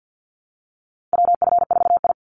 Morse code, heard as a keyed tone, about 48 wpm, 700 Hz. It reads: WF4I
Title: NA SOTA Chaser Callsigns 50wpm